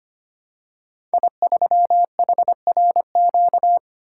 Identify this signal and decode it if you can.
Morse code, heard as a keyed tone, about 25 wpm, 700 Hz. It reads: I3HRQ
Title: Code Groups - 5 Letter:Numbers - 25wpm